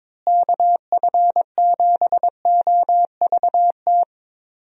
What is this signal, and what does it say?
Morse code, heard as a keyed tone, about 22 wpm, 700 Hz. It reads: KF7OVT
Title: USA Call Signs - 6 Characters - Repeated 2x - 22wpm